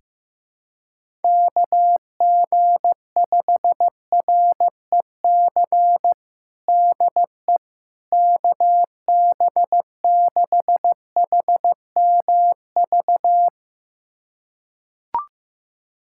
Morse code, heard as a keyed tone, about 15 wpm, 700 Hz. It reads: KG5REC DE KB6HMV
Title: USA Callsign DE Callsign - 6 Characters - 15wpm